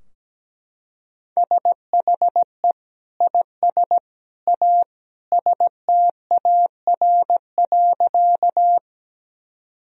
Morse code, heard as a keyed tone, about 17 wpm, 700 Hz. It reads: SHE IS A STAR.
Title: Sentences from Top 1000 Words - Review Part 2 - 17wpm